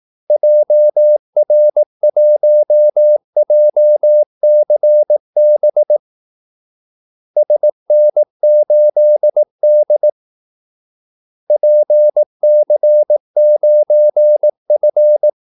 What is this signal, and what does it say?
Morse code, heard as a keyed tone, about 18 wpm, 600 Hz. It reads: JR1JCB SN8D PC9F
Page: Kaldesignaler 18 wpm